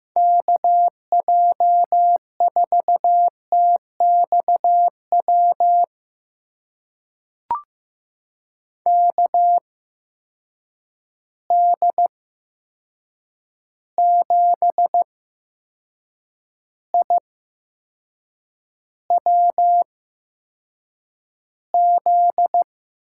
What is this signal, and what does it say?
Morse code, heard as a keyed tone, about 15 wpm, spaced as for 6 wpm, 700 Hz. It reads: KJ4TXW KD7IWZ
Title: USA Call Signs - 6 Characters - 8x Character Spacing - 15wpm